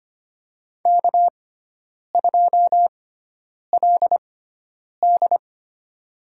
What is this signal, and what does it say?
Morse code, heard as a keyed tone, about 25 wpm, spaced as for 9 wpm, 700 Hz. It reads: K2LD
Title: USA Call Signs - 4 Characters - 6x Character Spacing - 25wpm